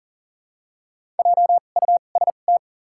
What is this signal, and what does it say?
Morse code, heard as a keyed tone, about 40 wpm, spaced as for 29 wpm, 700 Hz. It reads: JUST
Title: 2x Character Spacing - Top 300 Words - 40wpm